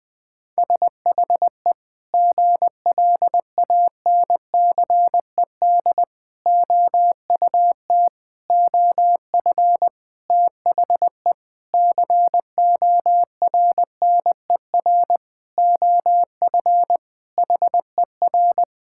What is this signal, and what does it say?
Morse code, heard as a keyed tone, about 20 wpm, 700 Hz. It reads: SHE GLANCED OUT OF THE CORNER OF HER
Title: Sentences with 5-Word Phrases - Spoken Beginning - 20wpm